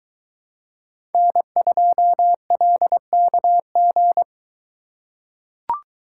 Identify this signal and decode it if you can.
Morse code, heard as a keyed tone, about 22 wpm, 700 Hz. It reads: N2LKG